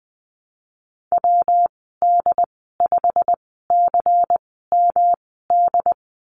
Morse code, heard as a keyed tone, about 20 wpm, spaced as for 15 wpm, 700 Hz. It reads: WD5CMD
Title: 179 Lesson 29g - Slash - Call Signs 20wpm